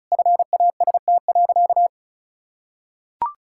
Morse code, heard as a keyed tone, about 36 wpm, 700 Hz. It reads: FAST.
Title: Sentences from Top 1600 Words 35wpm